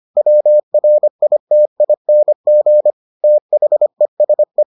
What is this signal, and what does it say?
Morse code, heard as a keyed tone, about 25 wpm, 600 Hz. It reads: WRITING THESE